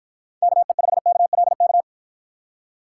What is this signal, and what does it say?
Morse code, heard as a keyed tone, about 52 wpm, 700 Hz. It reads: KE5KLK